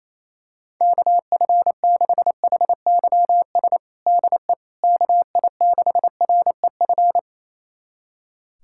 Morse code, heard as a keyed tone, about 28 wpm, 700 Hz. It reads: KF6HYS DE KI6REF
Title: USA Callsign DE Callsign - 6 Characters - 28wpm